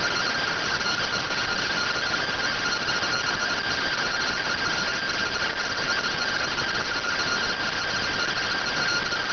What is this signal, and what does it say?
Two-tone teleprinter signal, about 170 Hz shift.